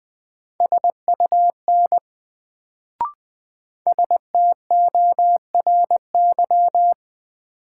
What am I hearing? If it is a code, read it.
Morse code, heard as a keyed tone, about 20 wpm, 700 Hz. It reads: SUN STORY
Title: Sentences from Top 300 Words - New Word Drill - 20wpm